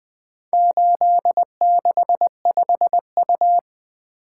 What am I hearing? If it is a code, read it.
Morse code, heard as a keyed tone, about 21 wpm, 700 Hz. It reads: 865U
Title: Four Letter-Number Combinations - 20wpm